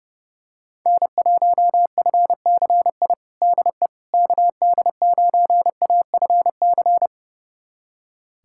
Morse code, heard as a keyed tone, about 30 wpm, 690 Hz. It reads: N1FCI DE KD9AFC
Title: USA Callsign DE Callsign 30wpm